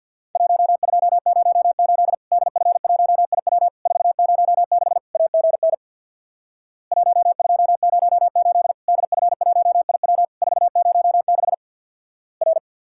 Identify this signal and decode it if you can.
Morse code, heard as a keyed tone, about 50 wpm, 700 Hz. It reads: 1208 DF1IW 406 AGN 1208 DF1IW 406 R
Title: Worked All Europe DX Contest - QTC Exchange with Repeats - 50wpm